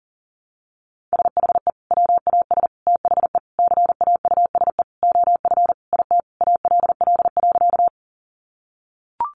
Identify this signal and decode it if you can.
Morse code, heard as a keyed tone, about 40 wpm, 700 Hz. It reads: SHE WAS THE CAUSE OF IT ALL.
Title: Sentences from Top 200 Words 40wpm